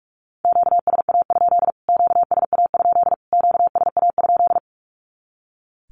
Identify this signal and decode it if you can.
Morse code, heard as a keyed tone, about 44 wpm, 700 Hz. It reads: QSA? QSA? QSA?
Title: Q-Codes Repeated 3x 45wpm